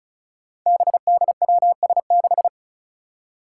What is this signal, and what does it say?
Morse code, heard as a keyed tone, about 36 wpm, 700 Hz. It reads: BDWS6
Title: Five Challenging Letter-Number Combinations 35wpm